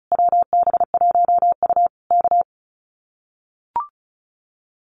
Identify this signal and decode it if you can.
Morse code, heard as a keyed tone, about 34 wpm, 700 Hz. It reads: WB1U K
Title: Common QSO Phrases Encore 35wpm